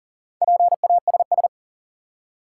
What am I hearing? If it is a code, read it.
Morse code, heard as a keyed tone, about 40 wpm, 700 Hz. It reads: PASS
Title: Four Letter Words - 40wpm